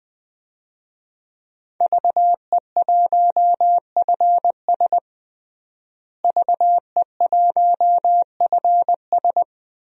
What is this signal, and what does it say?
Morse code, heard as a keyed tone, about 21 wpm, 700 Hz. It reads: VE1FS VE1FS
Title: DXCC-100 Call Signs - 5 Characters - Repeated 2x - 20wpm